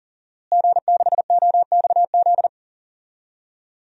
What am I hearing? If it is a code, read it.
Morse code, heard as a keyed tone, about 40 wpm, 700 Hz. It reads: G6OXZ
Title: DXCC-100 Call Signs - 5 Characters - Repeated 3x - 40wpm